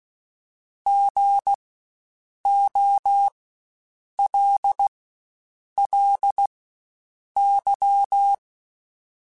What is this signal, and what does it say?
Morse code, heard as a keyed tone, about 16 wpm, spaced as for 7 wpm, 800 Hz. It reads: GOLLY
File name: Spaced Alphas / Spaced Alphas mp3s / Gs sp @15